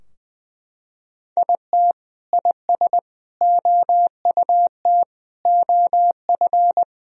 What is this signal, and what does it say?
Morse code, heard as a keyed tone, about 20 wpm, 700 Hz. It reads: IT IS OUT OF
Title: Sentences from Top 1000 Words - Review Part 4 - 20wpm